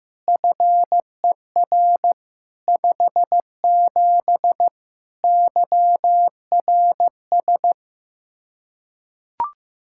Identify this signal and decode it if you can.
Morse code, heard as a keyed tone, about 15 wpm, 700 Hz. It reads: FER 57 YRS
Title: K1USN SST QSO Practice 15wpm